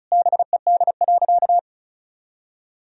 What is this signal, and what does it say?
Morse code, heard as a keyed tone, about 36 wpm, 700 Hz. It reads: BED.
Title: Sentences from Top 500 Words and Abbreviations 35wpm